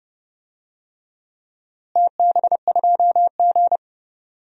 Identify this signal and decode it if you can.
Morse code, heard as a keyed tone, about 30 wpm, 700 Hz. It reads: TB2G
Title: DXCC-100 Call Signs - 4 Characters - Repeated 3x - 30wpm